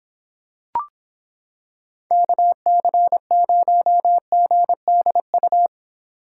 Morse code, heard as a keyed tone, about 26 wpm, 700 Hz. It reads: KC0GDU